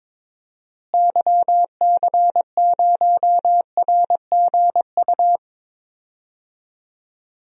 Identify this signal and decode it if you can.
Morse code, heard as a keyed tone, about 21 wpm, 700 Hz. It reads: YC0RGU
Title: DXCC-100 Call Signs - 6 Characters - Repeated 2x - 22wpm